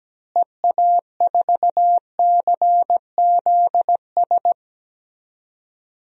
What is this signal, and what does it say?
Morse code, heard as a keyed tone, about 17 wpm, 700 Hz. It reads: EA4CZS
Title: DXCC-100 Call Signs - 6 Characters - Repeated 3x - 17wpm